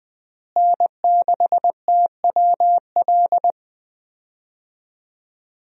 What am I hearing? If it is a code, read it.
Morse code, heard as a keyed tone, about 20 wpm, 700 Hz. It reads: N6TWL
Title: USA Call Signs - 5 Characters - Repeated 3x - 20wpm